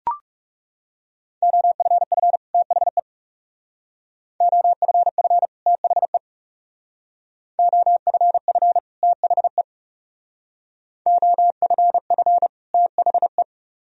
Morse code, heard as a keyed tone, about 40 wpm, 700 Hz. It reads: OFF THE OFF THE OFF THE OFF THE